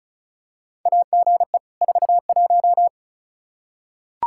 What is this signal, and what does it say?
Morse code, heard as a keyed tone, about 34 wpm, 700 Hz. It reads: AGE 41
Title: Practice QSO - Exchange 3 - More Info - 35wpm